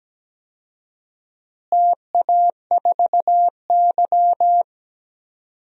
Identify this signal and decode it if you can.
Morse code, heard as a keyed tone, about 17 wpm, 700 Hz. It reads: TA4Y
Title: DXCC-100 Call Signs - 4 Characters - Repeated 3x - 17wpm